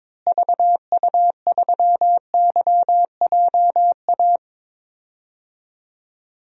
Morse code, heard as a keyed tone, about 21 wpm, 700 Hz. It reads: VU3YJA
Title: DXCC-100 Call Signs - 6 Characters - Repeated 2x - 22wpm